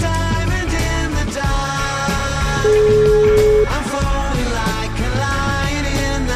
Ringback tone (Europe, 425 Hz), with noise.